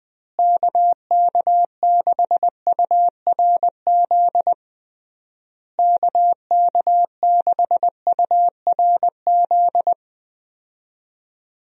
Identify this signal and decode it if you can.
Morse code, heard as a keyed tone, about 20 wpm, 700 Hz. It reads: KK6URZ KK6URZ
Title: USA Call Signs - 6 Characters - Repeated 3x - 20wpm